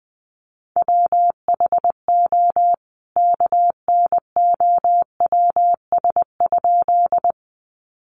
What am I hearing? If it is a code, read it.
Morse code, heard as a keyed tone, about 20 wpm, 700 Hz. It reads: WHO KNOWS?
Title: A Sound Of Thunder 20wpm